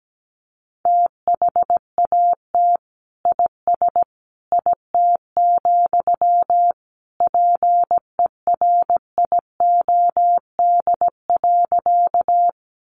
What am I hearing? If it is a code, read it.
Morse code, heard as a keyed tone, about 17 wpm, 700 Hz. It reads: THAT IS IT, PERIOD.
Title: Sentences from Top 800 Words 17wpm